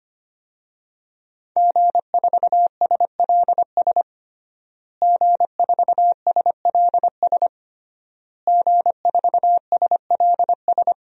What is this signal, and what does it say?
Morse code, heard as a keyed tone, about 25 wpm, 700 Hz. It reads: G4SLS G4SLS G4SLS
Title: DXCC-100 Call Signs - 5 Characters - Repeated 3x - 25wpm